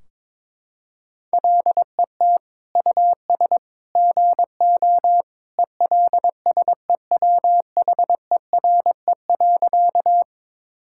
Morse code, heard as a keyed tone, about 22 wpm, 700 Hz. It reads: LET US GO ELSEWHERE.
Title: Sentences from Top 1600 Words 22wpm